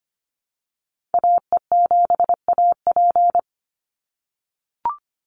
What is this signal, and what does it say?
Morse code, heard as a keyed tone, about 25 wpm, 700 Hz. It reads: AE7AP
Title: NA SOTA Chaser Callsigns 25wpm